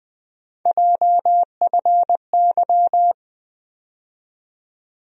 Morse code, heard as a keyed tone, about 20 wpm, 700 Hz. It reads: JFY
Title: USA Call Sign Postfixes - 3 Letters 20wpm